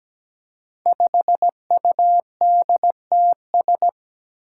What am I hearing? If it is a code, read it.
Morse code, heard as a keyed tone, about 17 wpm, 700 Hz. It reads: 5UDTS
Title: Five Challenging Letter-Number Combinations 17wpm